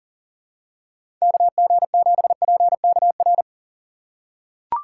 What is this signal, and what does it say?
Morse code, heard as a keyed tone, about 40 wpm, 700 Hz. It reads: KG7PKR
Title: USA Call Signs - 6 Characters - 3x Character Spacing - 40wpm